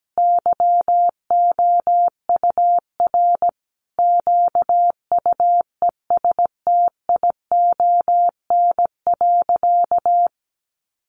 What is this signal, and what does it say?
Morse code, heard as a keyed tone, about 17 wpm, 700 Hz. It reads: YOUR QUESTION.
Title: Sentences from Top 800 Words 17wpm